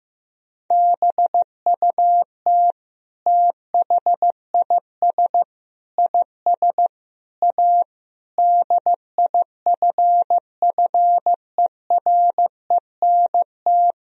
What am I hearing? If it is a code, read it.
Morse code, heard as a keyed tone, about 15 wpm, 700 Hz. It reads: BUT THIS IS A DIFFERENT
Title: Sentences from Top 1000 Words - Review Part 3 - 15wpm